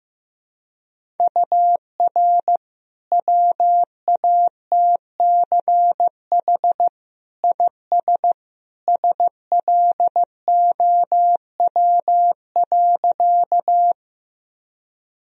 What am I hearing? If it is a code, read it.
Morse code, heard as a keyed tone, about 15 wpm, 700 Hz. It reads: UR WATCH IS SLOW.
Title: Sentences from Top 400 Words and Abbreviations 15wpm